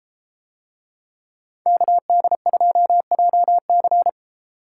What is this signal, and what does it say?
Morse code, heard as a keyed tone, about 31 wpm, 700 Hz. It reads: KD2JC